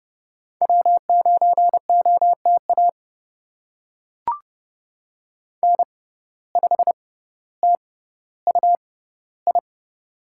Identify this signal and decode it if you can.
Morse code, heard as a keyed tone, about 31 wpm, spaced as for 16 wpm, 700 Hz. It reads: W9OTA N5TUI
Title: USA Call Signs - 5 Characters - 6x Character Spacing - 30wpm